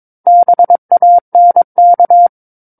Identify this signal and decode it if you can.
Morse code, heard as a keyed tone, about 22 wpm, 700 Hz. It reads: BANK
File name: Book One / 13-Chapter 13 - 22 WPM